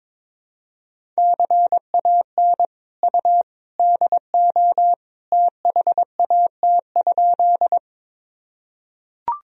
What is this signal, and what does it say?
Morse code, heard as a keyed tone, about 22 wpm, 700 Hz. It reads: CAN U DO THAT?
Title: Sentences from Top 100 Words and Abbreviations 22wpm